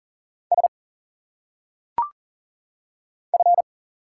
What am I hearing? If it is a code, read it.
Morse code, heard as a keyed tone, about 40 wpm, 700 Hz. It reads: S F
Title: Single Letters - Phonetic Pronunciation 40wpm